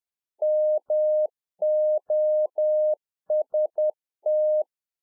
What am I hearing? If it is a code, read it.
Morse code, heard as a keyed tone, about 10 wpm, 610 Hz. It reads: MOST
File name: Czech Army Morse Keys 10WPM